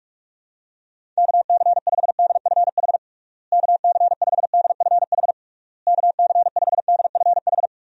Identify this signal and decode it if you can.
Morse code, heard as a keyed tone, about 46 wpm, 700 Hz. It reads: KC5DFH KC5DFH KC5DFH
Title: USA Call Signs - 6 Characters - Repeated 3x - 45wpm